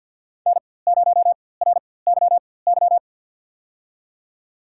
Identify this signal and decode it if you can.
Morse code, heard as a keyed tone, about 53 wpm, spaced as for 24 wpm, 700 Hz. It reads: N0RYY
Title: USA Call Signs - 5 Characters - 4x Character Spacing - 50wpm